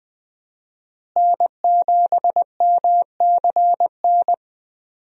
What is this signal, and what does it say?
Morse code, heard as a keyed tone, about 20 wpm, 700 Hz. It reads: N7MCN
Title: USA Call Signs - 5 Characters - Repeated 2x - 20wpm